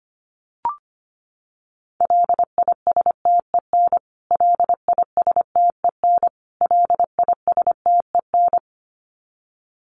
Morse code, heard as a keyed tone, about 25 wpm, 700 Hz. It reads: LISTEN LISTEN LISTEN
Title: Top 500 Words Repeated 25wpm